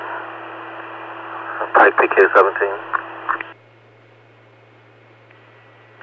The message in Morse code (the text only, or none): none